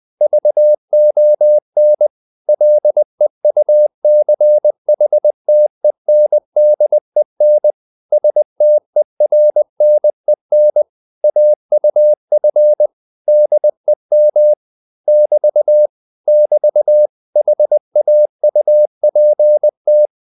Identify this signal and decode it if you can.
Morse code, heard as a keyed tone, about 20 wpm, 600 Hz. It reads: VON LEUCHTENDEN STERNEN AUF DEM <BT> <BT> HAUPT